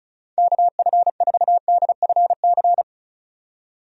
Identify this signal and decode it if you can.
Morse code, heard as a keyed tone, about 34 wpm, 700 Hz. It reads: KF4DFC
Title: USA Call Signs - 6 Characters - 35wpm